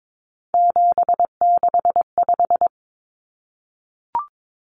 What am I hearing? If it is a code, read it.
Morse code, heard as a keyed tone, about 22 wpm, 700 Hz. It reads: 765
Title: Numbers - 3-Digit 22wpm